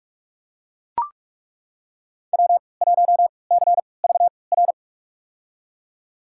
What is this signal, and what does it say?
Morse code, heard as a keyed tone, about 43 wpm, spaced as for 25 wpm, 700 Hz. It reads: W1CVR
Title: USA Call Signs - 5 Characters - 3x Character Spacing - 45wpm